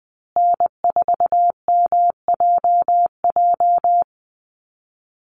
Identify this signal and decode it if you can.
Morse code, heard as a keyed tone, about 20 wpm, 700 Hz. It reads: N4MJJ
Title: USA Call Signs Repeated 3x 20wpm